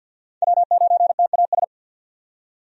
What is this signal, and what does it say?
Morse code, heard as a keyed tone, about 50 wpm, 700 Hz. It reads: W9TAS
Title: USA Call Signs - 5 Characters - 50wpm